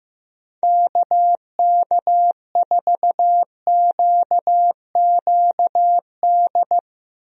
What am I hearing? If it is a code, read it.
Morse code, heard as a keyed tone, about 16 wpm, 700 Hz. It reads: KK4QQD
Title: USA Call Signs - 6 Characters - Repeated 2x - 15wpm